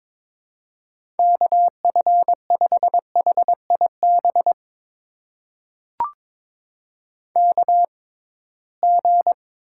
Morse code, heard as a keyed tone, about 23 wpm, spaced as for 17 wpm, 700 Hz. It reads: KF5HIB KG
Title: USA Call Signs - 6 Characters - 6x Character Spacing - 22wpm